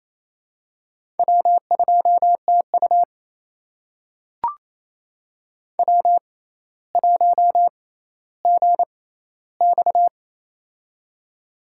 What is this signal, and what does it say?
Morse code, heard as a keyed tone, about 28 wpm, spaced as for 14 wpm, 700 Hz. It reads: W2TU W1GX
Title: USA Call Signs - 4 Characters - 6x Character Spacing - 28wpm